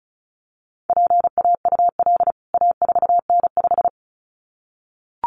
Morse code, heard as a keyed tone, about 34 wpm, 700 Hz. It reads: PAUL A4N5
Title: CWT Exchange 35wpm